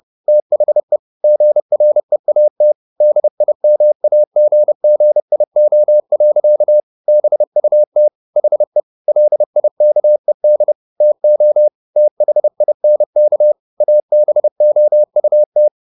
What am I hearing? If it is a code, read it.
Morse code, heard as a keyed tone, about 30 wpm, 600 Hz. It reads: THE GREAT DIMAGGIO. BUT HE LIKED TO THINK ABOUT